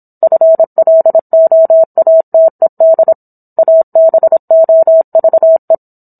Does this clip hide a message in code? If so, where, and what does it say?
Yes: Morse code from 0.2 to 5.8 s, heard as a keyed tone, about 27 wpm, 650 Hz. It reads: FLOATED ABOVE